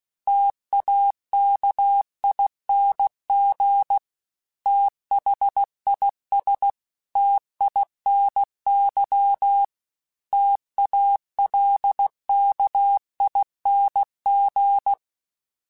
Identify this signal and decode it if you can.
Morse code, heard as a keyed tone, about 16 wpm, 800 Hz. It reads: TAKING THIS TINY TALKING
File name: T Runs / T Runs mp3s / T Runs @15